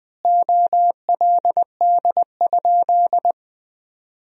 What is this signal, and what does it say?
Morse code, heard as a keyed tone, about 20 wpm, 700 Hz. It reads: OLD?
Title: 2x Word-Spacing - Sentences from Top 400 Words - 20wpm